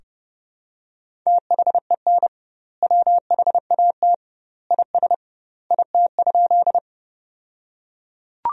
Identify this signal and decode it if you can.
Morse code, heard as a keyed tone, about 30 wpm, 700 Hz. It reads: THEN WHAT IS IT?
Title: 2x Word-Spacing - Sentences from Top 100 Words - 30wpm